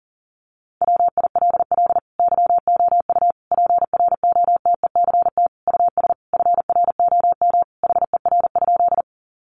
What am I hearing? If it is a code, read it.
Morse code, heard as a keyed tone, about 40 wpm, 700 Hz. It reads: WILL YOU PROTECT US FROM HER?
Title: Sentences from Top 900 Words 40wpm